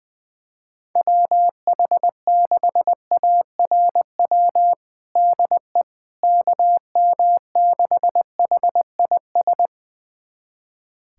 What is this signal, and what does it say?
Morse code, heard as a keyed tone, about 21 wpm, 700 Hz. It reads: WH6ARW DE KM6HIS
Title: USA Callsign DE Callsign - 6 Characters - 20wpm